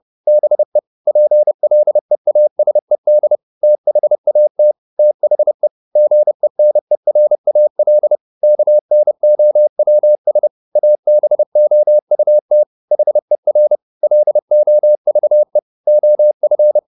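Morse code, heard as a keyed tone, about 30 wpm, 600 Hz. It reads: BE PLEASED THAT THE GENERAL KNOWS ABOUT HER LOVE OF
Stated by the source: Morse code at 30 wpm, the novel Never by Ken Follett in English, from CW med Gnister.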